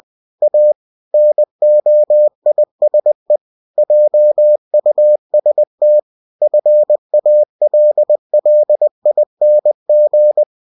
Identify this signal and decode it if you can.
Morse code, heard as a keyed tone, about 20 wpm, 600 Hz. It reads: A NOISE JUST FALLING
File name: cw-books / winnie-the-pooh / winnie-20wpm-page-0053